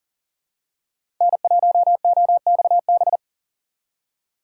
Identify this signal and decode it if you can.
Morse code, heard as a keyed tone, about 40 wpm, 700 Hz. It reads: N1OXB
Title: USA Call Signs - 5 Characters - 4x Character Spacing - 40wpm